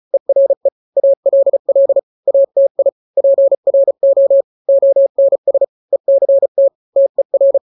Morse code, heard as a keyed tone, about 34 wpm, 550 Hz. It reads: ERE ALL ATI PRO ONS ECT TER
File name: 3-grams 35 30 WPM